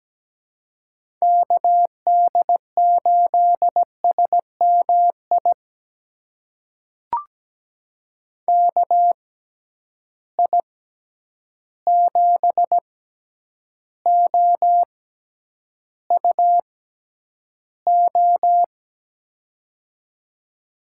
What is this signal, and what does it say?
Morse code, heard as a keyed tone, about 17 wpm, spaced as for 9 wpm, 700 Hz. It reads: KD8SMI KI7OUO
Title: USA Call Signs - 6 Characters - 6x Character Spacing - 17wpm